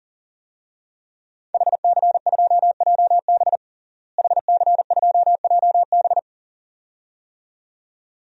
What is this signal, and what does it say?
Morse code, heard as a keyed tone, about 37 wpm, 700 Hz. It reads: HC2JB HC2JB
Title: DXCC-100 Call Signs - 5 Characters - Repeated 2x - 40wpm